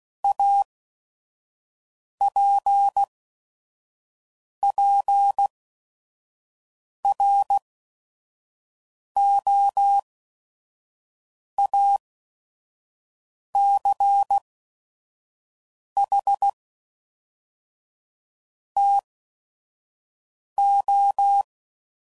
Morse code, heard as a keyed tone, about 16 wpm, spaced as for 5 wpm, 800 Hz. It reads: APPROACH TO
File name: CW Code Groups @5WPM / Technical Subjects / Gates Intro